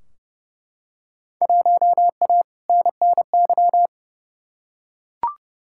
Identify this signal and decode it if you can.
Morse code, heard as a keyed tone, about 31 wpm, 700 Hz. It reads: 1A NNY
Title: Field Day Exchanges - 30wpm